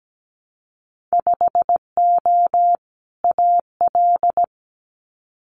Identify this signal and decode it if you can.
Morse code, heard as a keyed tone, about 17 wpm, 700 Hz. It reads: 5O AL
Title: Winter Field Day Exchanges 17wpm